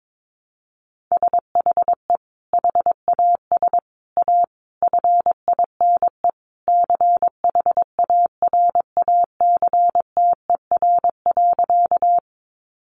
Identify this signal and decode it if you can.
Morse code, heard as a keyed tone, about 22 wpm, 700 Hz. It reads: SHE HAS A FINE CHARACTER.
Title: Sentences from Top 1200 Words 22wpm